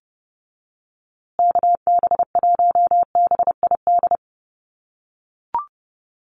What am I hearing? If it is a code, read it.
Morse code, heard as a keyed tone, about 30 wpm, 700 Hz. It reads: KB1BID
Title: USA Call Signs Repeated 2x 30wpm